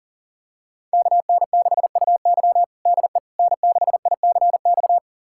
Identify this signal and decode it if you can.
Morse code, heard as a keyed tone, about 40 wpm, 700 Hz. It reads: KN6UY DE N6ICX
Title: USA Callsign DE Callsign - 5 Characters - 40wpm